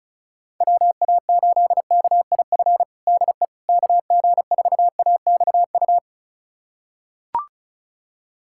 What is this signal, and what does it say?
Morse code, heard as a keyed tone, about 34 wpm, 700 Hz. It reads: WA8KIF DE KG4AXU
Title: USA Callsign DE Callsign - 6 Characters - 35wpm